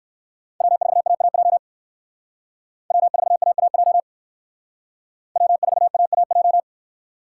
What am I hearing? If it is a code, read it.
Morse code, heard as a keyed tone, about 58 wpm, 700 Hz. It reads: W4AAJ W4AAJ W4AAJ